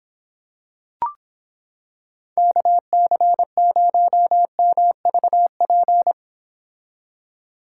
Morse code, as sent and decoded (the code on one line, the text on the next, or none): -.- -.-. ----- -- ...- .--.
KC0MVP